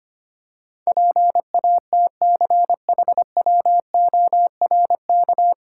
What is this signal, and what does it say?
Morse code, heard as a keyed tone, about 25 wpm, 700 Hz. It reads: PATCHWORK
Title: Pangram Challenge v2 - 25wpm